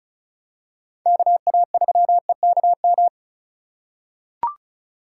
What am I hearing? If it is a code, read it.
Morse code, heard as a keyed tone, about 36 wpm, 700 Hz. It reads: KA3EKM
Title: USA Call Signs - 6 Characters - 4x Character Spacing - 35wpm